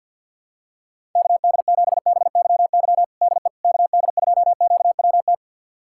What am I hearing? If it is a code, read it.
Morse code, heard as a keyed tone, about 46 wpm, 700 Hz. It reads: KD7BYY DE KD2QWT
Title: USA Callsign DE Callsign - 6 Characters - 50wpm